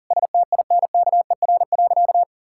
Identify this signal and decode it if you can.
Morse code, heard as a keyed tone, about 37 wpm, 700 Hz. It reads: STINKER.
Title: Pangram Challenge v2 - 40wpm